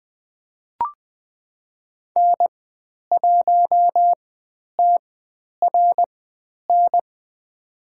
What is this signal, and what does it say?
Morse code, heard as a keyed tone, about 20 wpm, spaced as for 10 wpm, 700 Hz. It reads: N1TRN
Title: 095 Lesson 16g - Letter c - Call Signs 20wpm